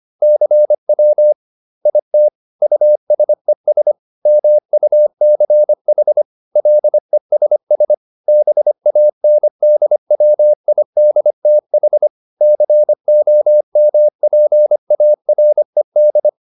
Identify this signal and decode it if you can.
Morse code, heard as a keyed tone, about 25 wpm, 600 Hz. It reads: CW IT USES MUCH LESS BANDWIDTH COMPARED